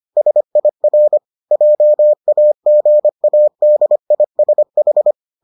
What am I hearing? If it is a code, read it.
Morse code, heard as a keyed tone, about 25 wpm, 600 Hz. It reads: SIR JAGADISH